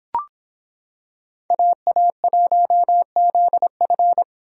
Morse code, heard as a keyed tone, about 25 wpm, 700 Hz. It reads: AA1ZF